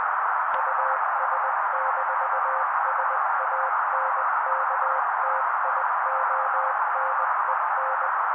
Morse code, heard as a keyed tone, about 20 wpm, 530 Hz. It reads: US-SANKTIONEN
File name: daily / 18wpm noise